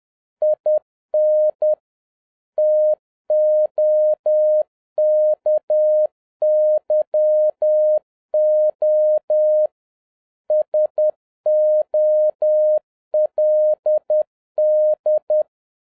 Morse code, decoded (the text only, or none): IN TOKYO SOLD